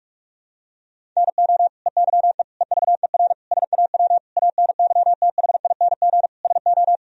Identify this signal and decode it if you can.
Morse code, heard as a keyed tone, about 46 wpm, 700 Hz. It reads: NO EYE EVER SAW ANYTHING SO
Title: Sentences from Top 1000 Words - Review Part 1 - 45wpm